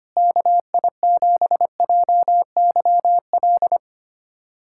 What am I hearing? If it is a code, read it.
Morse code, heard as a keyed tone, about 25 wpm, 700 Hz. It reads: KI7JYL
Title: USA Callsign DE Callsign 25wpm